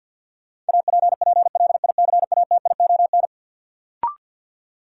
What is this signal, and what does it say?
Morse code, heard as a keyed tone, about 52 wpm, 700 Hz. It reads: APPLICATION
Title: Sentences from Top 1200 Words - New Word Drill Alt - 50wpm